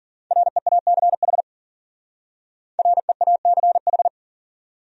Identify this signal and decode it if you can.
Morse code, heard as a keyed tone, about 52 wpm, 700 Hz. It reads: REACH REACH